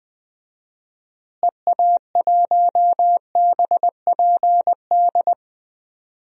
Morse code, heard as a keyed tone, about 20 wpm, 700 Hz. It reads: EA1BPD
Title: DXCC-100 Call Signs - 6 Characters - Repeated 2x - 20wpm